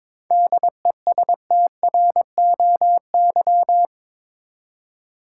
Morse code, heard as a keyed tone, about 22 wpm, 700 Hz. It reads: DESTROY
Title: Seven Letter Words - 22wpm